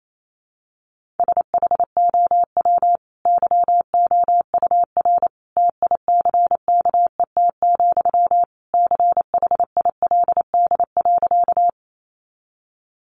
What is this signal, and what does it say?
Morse code, heard as a keyed tone, about 28 wpm, 700 Hz. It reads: SHOW YOUR TICKET, CHILD.
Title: Sentences from Top 1400 Words 28wpm